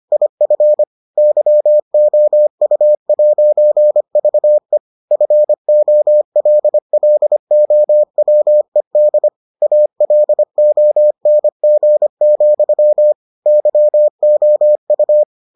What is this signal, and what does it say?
Morse code, heard as a keyed tone, about 25 wpm, 600 Hz. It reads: IF YOU'VE FOLLOWED ALONG, YOU